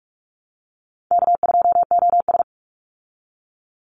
Morse code, heard as a keyed tone, about 44 wpm, 700 Hz. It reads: K2OS
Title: Easy Call Signs Repeated 3x - 45wpm